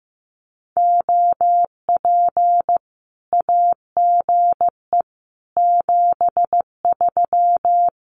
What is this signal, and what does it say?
Morse code, heard as a keyed tone, about 15 wpm, 700 Hz. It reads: OP AGE 73
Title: Common QSO Phrases 15wpm